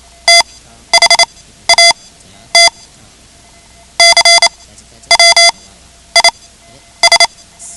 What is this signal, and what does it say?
Morse code, heard as a keyed tone, about 28 wpm, spaced as for 12 wpm, 2050 Hz. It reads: THAT CW IS
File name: ham radio / CodeQuick / Code Quick 10